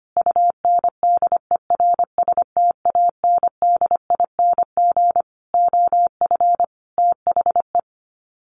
Morse code, heard as a keyed tone, about 25 wpm, 700 Hz. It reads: UNDERSTANDING OF THE
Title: Sets of 5 Words Encore2 25wpm